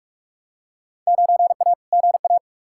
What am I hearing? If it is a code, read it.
Morse code, heard as a keyed tone, about 41 wpm, 700 Hz. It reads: 9A GA
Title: Field Day Exchanges Encore - 45wpm